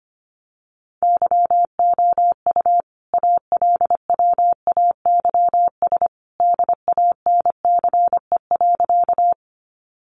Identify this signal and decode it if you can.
Morse code, heard as a keyed tone, about 25 wpm, 700 Hz. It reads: YOU ALWAYS DANCE.
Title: Sentences from Top 700 Words 25wpm